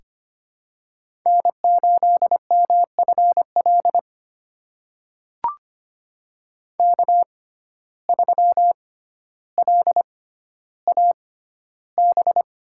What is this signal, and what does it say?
Morse code, heard as a keyed tone, about 25 wpm, spaced as for 13 wpm, 700 Hz. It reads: N8MFL K3LAB
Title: USA Call Signs - 5 Characters - 6x Character Spacing - 25wpm